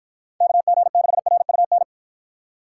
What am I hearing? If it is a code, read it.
Morse code, heard as a keyed tone, about 54 wpm, 700 Hz. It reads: KG6RUN